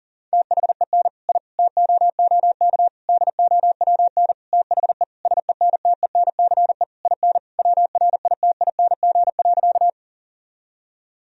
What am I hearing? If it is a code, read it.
Morse code, heard as a keyed tone, about 40 wpm, 700 Hz. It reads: THEN I TOOK DOWN THE SENTENCE IN WRITING.
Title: Sentences from Top 1000 Words - Review Part 1 - 40wpm